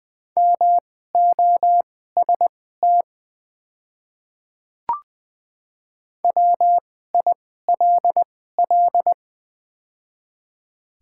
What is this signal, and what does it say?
Morse code, heard as a keyed tone, about 20 wpm, spaced as for 15 wpm, 700 Hz. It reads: MOST WILL
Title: 2x Character Spacing - Top 100 Words - 20wpm